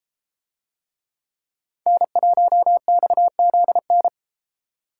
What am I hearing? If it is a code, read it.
Morse code, heard as a keyed tone, about 31 wpm, 700 Hz. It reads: N1XZN